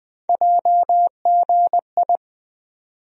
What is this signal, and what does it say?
Morse code, heard as a keyed tone, about 20 wpm, 700 Hz. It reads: JGI
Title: USA Call Sign Postfixes - 3 Letters 20wpm